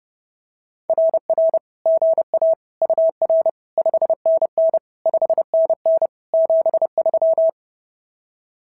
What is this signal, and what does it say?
Morse code, heard as a keyed tone, about 30 wpm, 690 Hz. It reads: RR GA UR 5NN 5NN 73
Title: SOTA QSO Practice 30wpm